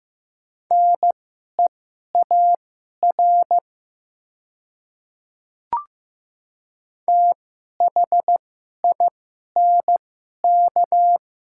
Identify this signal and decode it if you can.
Morse code, heard as a keyed tone, about 15 wpm, spaced as for 11 wpm, 700 Hz. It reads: NEAR THINK
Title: 2x Character Spacing - Top 500 Words - 15wpm